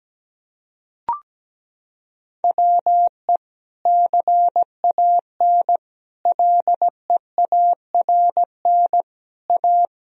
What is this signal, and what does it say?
Morse code, heard as a keyed tone, about 17 wpm, 700 Hz. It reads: WE CAN LEARN A
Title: Sentences from Top 1000 Words - Review Part 2 - 17wpm